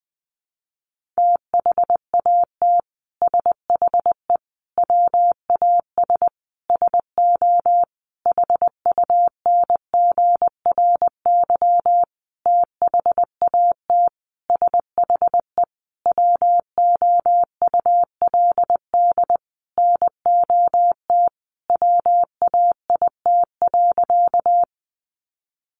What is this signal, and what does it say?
Morse code, heard as a keyed tone, about 20 wpm, 700 Hz. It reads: THAT SHE WAS SO HUNGRY THAT SHE WOULD NOT WAIT.
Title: Sentences from Top 1400 Words 20wpm